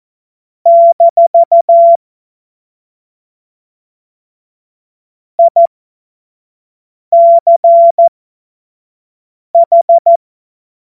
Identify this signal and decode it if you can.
Morse code, heard as a keyed tone, about 14 wpm, spaced as for 5 wpm, 680 Hz. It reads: - ICH
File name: Witz-08-Juli-2025-25-F70